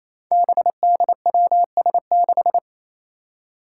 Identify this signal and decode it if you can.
Morse code, heard as a keyed tone, about 28 wpm, 700 Hz. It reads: BDWS6
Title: Five Challenging Letter-Number Combinations 28wpm